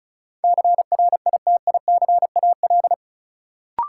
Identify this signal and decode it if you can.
Morse code, heard as a keyed tone, about 36 wpm, 700 Hz. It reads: CRITICAL
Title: Sentences from Top 1100 Words - New Word Drill - 35wpm